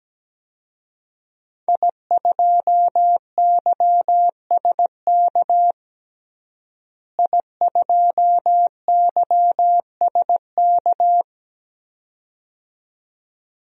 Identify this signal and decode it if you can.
Morse code, heard as a keyed tone, about 17 wpm, 700 Hz. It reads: I2YSK I2YSK
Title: DXCC-100 Call Signs - 5 Characters - Repeated 2x - 17wpm